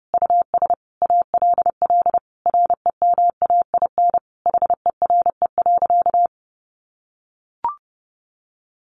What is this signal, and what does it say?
Morse code, heard as a keyed tone, about 30 wpm, 700 Hz. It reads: US ALL REMAIN HERE.
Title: Sentences from Top 1200 Words 30wpm